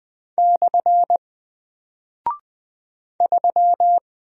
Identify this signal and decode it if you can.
Morse code, heard as a keyed tone, about 20 wpm, 700 Hz. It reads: / 3
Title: Single Character (Letter:Number:Punctuation:Prosign) Alt 20wpm